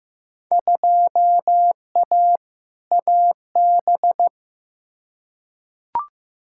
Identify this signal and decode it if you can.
Morse code, heard as a keyed tone, about 16 wpm, 700 Hz. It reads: 2A AB
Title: Field Day Exchanges - 15wpm